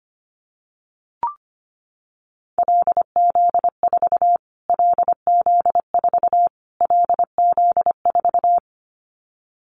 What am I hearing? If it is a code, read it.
Morse code, heard as a keyed tone, about 25 wpm, 700 Hz. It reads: LZ4 LZ4 LZ4
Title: Callsign Prefixes for Top 25 DXCC - Repeated 3x - 25wpm